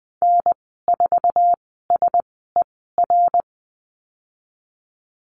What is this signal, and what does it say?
Morse code, heard as a keyed tone, about 20 wpm, spaced as for 14 wpm, 700 Hz. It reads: N4SER
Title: 055 Lesson 10g - Letter r - Call Signs 20wpm